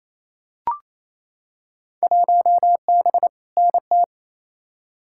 Morse code, heard as a keyed tone, about 28 wpm, 700 Hz. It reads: 1B NT
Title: Field Day Exchanges Encore - 28wpm